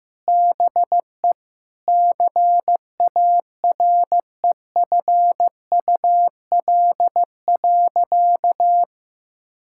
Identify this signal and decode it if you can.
Morse code, heard as a keyed tone, about 15 wpm, 700 Hz. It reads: BE CAREFUL.
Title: Sentences from Top 1600 Words 15wpm